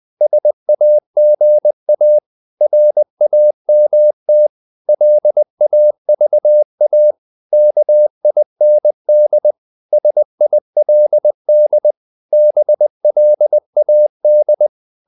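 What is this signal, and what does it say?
Morse code, heard as a keyed tone, about 20 wpm, 600 Hz. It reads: SAGA RAMT LAVA KIND SILD BLAD
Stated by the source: Morse code practice of short Danish words at 20 wpm, from CW med Gnister.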